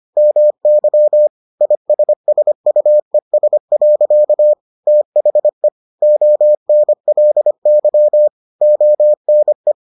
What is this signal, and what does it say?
Morse code, heard as a keyed tone, about 25 wpm, 600 Hz. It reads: MY ISSUES. THE ONLY ONE